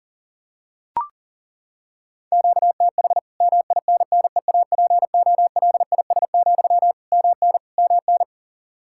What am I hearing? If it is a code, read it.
Morse code, heard as a keyed tone, about 40 wpm, 700 Hz. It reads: QTH MINNEAPOLIS, MN MN
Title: Practice QSO - Exchange 1 - Beginning - 40wpm